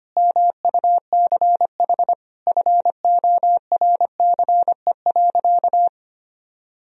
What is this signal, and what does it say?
Morse code, heard as a keyed tone, about 25 wpm, 700 Hz. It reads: MUCH FORCE.
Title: Sentences from Top 1000 Words - Review Part 3 - 25wpm